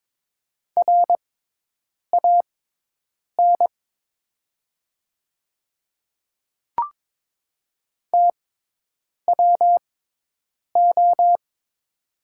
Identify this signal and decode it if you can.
Morse code, heard as a keyed tone, about 22 wpm, spaced as for 8 wpm, 700 Hz. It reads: RAN TWO
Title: 6x Character Spacing - Top 500 Words - 22wpm